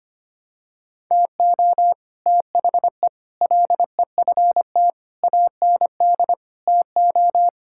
Morse code, heard as a keyed tone, about 25 wpm, 700 Hz. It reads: TO THE LEFT AND TO
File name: Sets of 9 Words - 25wpm